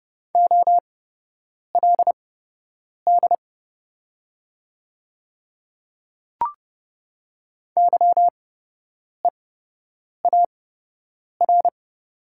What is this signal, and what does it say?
Morse code, heard as a keyed tone, about 30 wpm, spaced as for 8 wpm, 700 Hz. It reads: OLD YEAR
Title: 8x Character Spacing - Top 300 Words - 30wpm